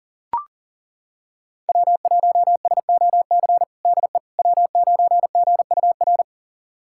Morse code, heard as a keyed tone, about 40 wpm, 700 Hz. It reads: W1SOC DE W9GUR
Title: USA Callsign DE Callsign - 5 Characters - 40wpm